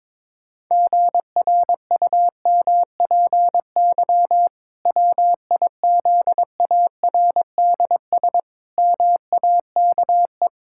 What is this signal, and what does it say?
Morse code, heard as a keyed tone, about 22 wpm, 700 Hz. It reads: GRUMPY WIZARDS MAKE
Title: Pangram Challenge v2 - 22wpm